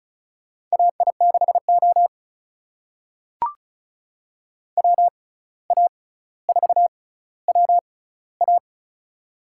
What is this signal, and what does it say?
Morse code, heard as a keyed tone, about 36 wpm, spaced as for 17 wpm, 700 Hz. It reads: AI6O WA4WA
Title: USA Call Signs - 6x Character Spacing - 35wpm